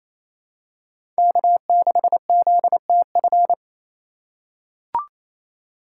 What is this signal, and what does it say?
Morse code, heard as a keyed tone, about 28 wpm, 700 Hz. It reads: K6ZTF
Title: USA Call Signs - 5 Characters - 4x Character Spacing - 28wpm